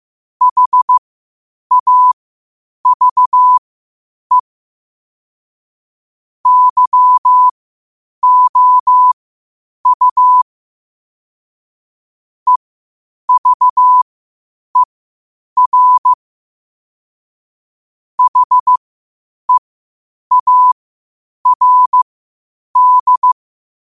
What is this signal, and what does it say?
Morse code, heard as a keyed tone, about 15 wpm, spaced as for 8 wpm, 1000 Hz. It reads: HAVE YOU EVER HEARD
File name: qotdfiles / 8wpm